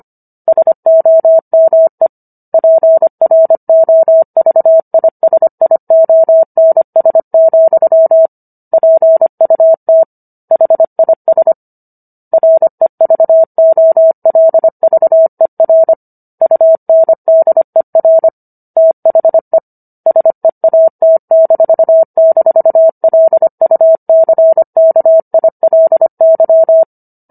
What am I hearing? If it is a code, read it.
Morse code, heard as a keyed tone, about 25 wpm, 650 Hz. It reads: SOME PROVISIONS, PUT HIS REVOLVER UNDER THE SEAT--LUCKILY
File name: Book One / 16-Chapter 16 - 25 WPM